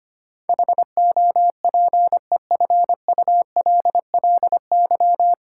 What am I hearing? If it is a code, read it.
Morse code, heard as a keyed tone, about 25 wpm, 700 Hz. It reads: HOPEFULLY
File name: Nine Letter Words - 25wpm